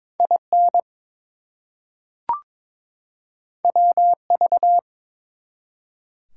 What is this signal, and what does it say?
Morse code, heard as a keyed tone, about 22 wpm, 700 Hz. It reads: IN WV
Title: US States and Canadian Provinces 22wpm